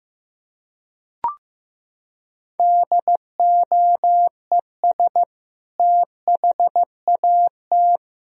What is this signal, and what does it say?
Morse code, heard as a keyed tone, about 15 wpm, 700 Hz. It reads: DOES THAT
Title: Sentences from Top 1000 Words - Review Part 1 - 15wpm